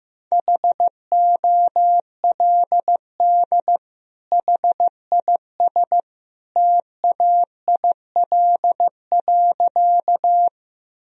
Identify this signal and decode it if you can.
Morse code, heard as a keyed tone, about 15 wpm, 700 Hz. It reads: HOLD HIS TAIL.
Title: Sentences from Top 1000 Words - Review Part 3 - 15wpm